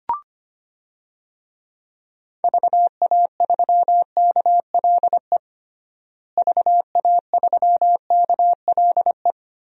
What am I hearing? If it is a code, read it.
Morse code, heard as a keyed tone, about 25 wpm, 700 Hz. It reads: VA3KLE VA3KLE
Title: DXCC-100 Call Signs - 6 Characters - Repeated 3x - 25wpm